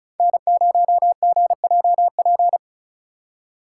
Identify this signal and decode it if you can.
Morse code, heard as a keyed tone, about 36 wpm, 700 Hz. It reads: N0GJP
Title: USA Call Signs - 5 Characters - 4x Character Spacing - 35wpm